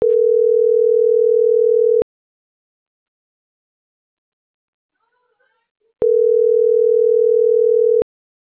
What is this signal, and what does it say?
Ringback tone (North America), 440 + 480 Hz.